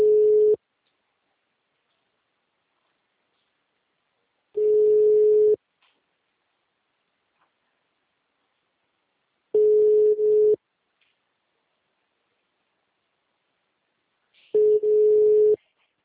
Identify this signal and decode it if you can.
Ringback tone (Europe, 425 Hz).